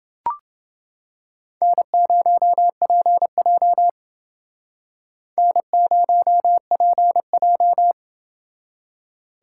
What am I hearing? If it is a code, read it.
Morse code, heard as a keyed tone, about 29 wpm, 700 Hz. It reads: N0PJ N0PJ